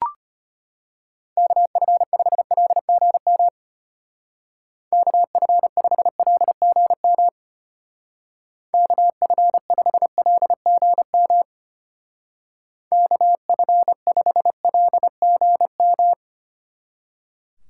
Morse code, heard as a keyed tone, about 30 wpm, 700 Hz. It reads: KF5LGM KF5LGM KF5LGM KF5LGM
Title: Speed Racing - USA 2x3 Call Signs 25wpm